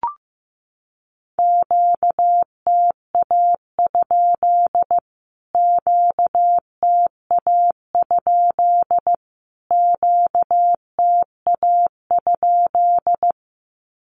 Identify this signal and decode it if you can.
Morse code, heard as a keyed tone, about 15 wpm, 700 Hz. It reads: QTA? QTA? QTA?
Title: Q-Codes Repeated 3x 15wpm